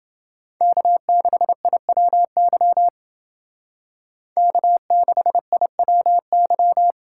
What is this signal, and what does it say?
Morse code, heard as a keyed tone, about 28 wpm, 700 Hz. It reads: K6IWY K6IWY